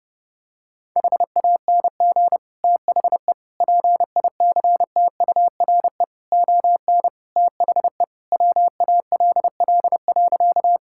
Morse code, heard as a keyed tone, about 32 wpm, 700 Hz. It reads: HANG THE PICTURE ON THE WALL.
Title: Sentences from Top 1600 Words 30wpm